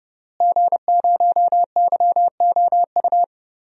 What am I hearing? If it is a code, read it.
Morse code, heard as a keyed tone, about 30 wpm, 700 Hz. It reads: G0YOU
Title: DXCC-100 Call Signs - 5 Characters - Repeated 2x - 30wpm